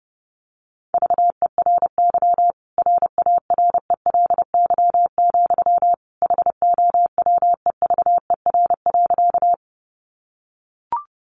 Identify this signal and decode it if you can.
Morse code, heard as a keyed tone, about 30 wpm, 700 Hz. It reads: VERY RARELY, HOWEVER.
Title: Sentences from Top 1400 Words 30wpm